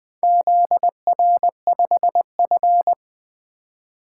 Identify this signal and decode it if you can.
Morse code, heard as a keyed tone, about 20 wpm, 700 Hz. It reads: ZR5F
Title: Four Letter-Number Combinations - 20wpm